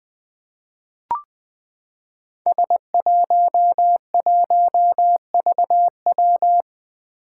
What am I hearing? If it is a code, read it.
Morse code, heard as a keyed tone, about 20 wpm, 700 Hz. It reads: S11VW
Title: Five Challenging Letter-Number Combinations 20wpm